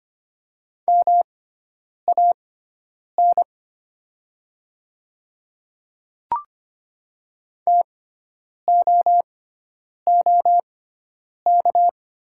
Morse code, heard as a keyed tone, about 25 wpm, spaced as for 9 wpm, 700 Hz. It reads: MAN TOOK
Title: 6x Character Spacing - Top 500 Words - 25wpm